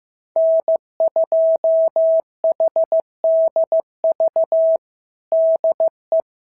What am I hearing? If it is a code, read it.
Morse code, heard as a keyed tone, about 14 wpm, 650 Hz. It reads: N2HDV DE
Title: SOTA QSO Practice 15wpm